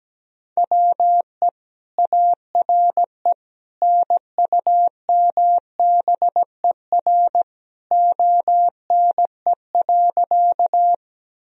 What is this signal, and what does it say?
Morse code, heard as a keyed tone, about 17 wpm, 700 Hz. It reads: WE ARE NUMBER ONE.
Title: Sentences from Top 1000 Words - Review Part 1 - 17wpm